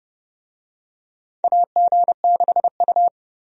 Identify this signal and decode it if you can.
Morse code, heard as a keyed tone, about 32 wpm, 700 Hz. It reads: AG6U
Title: USA Call Signs - 4 Characters - 30wpm